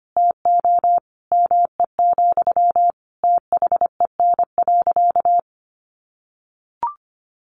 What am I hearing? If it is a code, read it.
Morse code, heard as a keyed tone, about 25 wpm, 700 Hz. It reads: TO ME, THEN.
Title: Sentences from Top 1100 Words 25wpm